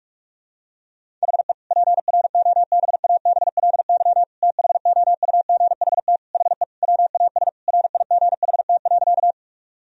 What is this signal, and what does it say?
Morse code, heard as a keyed tone, about 46 wpm, 700 Hz. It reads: HE PROBABLY THOUGHT HE WAS RIGHT.
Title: Sentences from Top 1000 Words - Review Part 4 - 45wpm